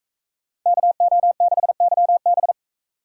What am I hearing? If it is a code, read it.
Morse code, heard as a keyed tone, about 40 wpm, 700 Hz. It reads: KO6YB